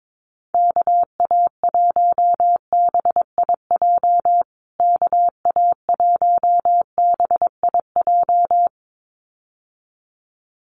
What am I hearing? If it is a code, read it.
Morse code, heard as a keyed tone, about 22 wpm, 700 Hz. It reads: KA1BIJ KA1BIJ
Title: USA Call Signs Repeated 2x 22wpm